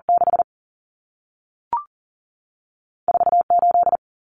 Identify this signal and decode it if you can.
Morse code, heard as a keyed tone, about 40 wpm, 700 Hz. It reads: 6 4 8
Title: Common Cut Numbers - 2-Digit 40wpm